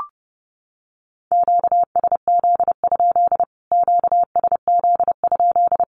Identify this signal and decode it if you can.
Morse code, heard as a keyed tone, about 30 wpm, 700 Hz. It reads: QSZ? QSZ?
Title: Q-Codes Repeated 3x 30wpm